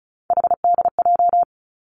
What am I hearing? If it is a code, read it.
Morse code, heard as a keyed tone, about 36 wpm, 700 Hz. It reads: HDJ
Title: Three Challenging Letter-Number Combinations 35wpm